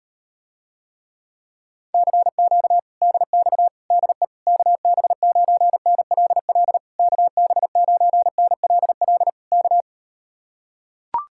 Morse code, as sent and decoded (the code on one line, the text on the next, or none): -.-. --.- / -.. -..- / -.. / -.- -... ----. -. .-.. .-.. / -.- -... ----. -. .-.. .-.. / -.-
CQ DX DE KB9NLL KB9NLL K